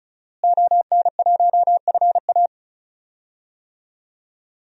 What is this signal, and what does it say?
Morse code, heard as a keyed tone, about 34 wpm, 700 Hz. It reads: ON1FA
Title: DXCC-100 Call Signs - 5 Characters - Repeated 2x - 35wpm